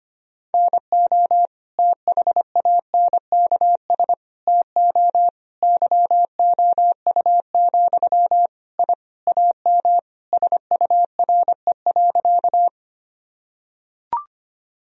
Morse code, heard as a keyed tone, about 25 wpm, 700 Hz. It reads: NO THANKS TO YOU, I AM SURE.
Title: Sentences from Top 1000 Words - Review Part 3 - 25wpm